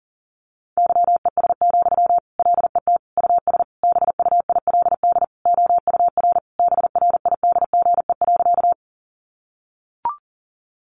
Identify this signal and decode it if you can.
Morse code, heard as a keyed tone, about 40 wpm, 700 Hz. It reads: YES, LET US BUILD OUR BRIDGE.
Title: Sentences from Top 1200 Words 40wpm